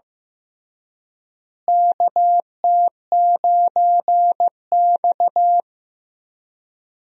Morse code, heard as a keyed tone, about 15 wpm, 700 Hz. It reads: KT9X
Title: USA Call Signs - 4 Characters - Repeated 2x - 15wpm